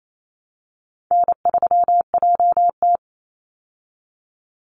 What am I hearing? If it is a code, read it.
Morse code, heard as a keyed tone, about 28 wpm, 700 Hz. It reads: N3JT
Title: Easy Call Signs - 28wpm